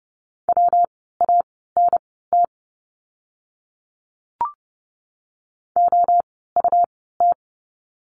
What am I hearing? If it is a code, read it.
Morse code, heard as a keyed tone, about 30 wpm, spaced as for 17 wpm, 700 Hz. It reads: WANT OUT
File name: course / 30:10wpm / 230 Lesson 37f - Letter x - Words 30wpm